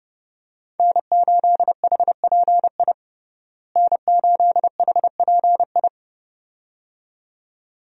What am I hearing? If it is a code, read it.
Morse code, heard as a keyed tone, about 30 wpm, 700 Hz. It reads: N8HPI N8HPI
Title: USA Call Signs - 5 Characters - Repeated 2x - 30wpm